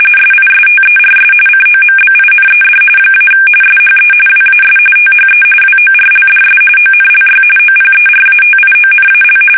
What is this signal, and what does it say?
RTTY, about 45.45 baud. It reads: SISTERS COMBINED - SHE WAS THE WORLDS ONLY OPERATIONAL
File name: fsk850 / battleshipnj